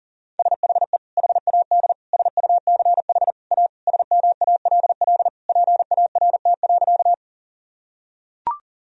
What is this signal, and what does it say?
Morse code, heard as a keyed tone, about 40 wpm, 700 Hz. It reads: SHE HAD SUCH A SMALL PART.
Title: Sentences from Top 1000 Words - Review Part 1 - 40wpm